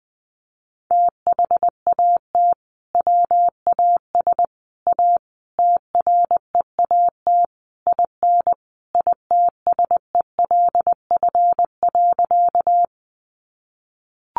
Morse code, heard as a keyed tone, about 20 wpm, 700 Hz. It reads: THAT WAS A TREAT IN ITSELF.
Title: Sentences from Top 1300 Words 20wpm